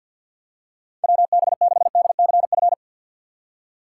Morse code, heard as a keyed tone, about 46 wpm, 700 Hz. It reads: WB6DCF
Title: USA Call Signs - 6 Characters - 4x Character Spacing - 50wpm